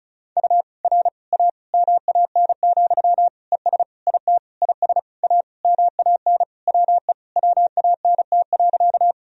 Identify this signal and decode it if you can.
Morse code, heard as a keyed tone, about 34 wpm, 700 Hz. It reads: U R A MAN, ES IT IS A MAN WE WANT.
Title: Sentences from Top 200 Words and Abbreviations 35wpm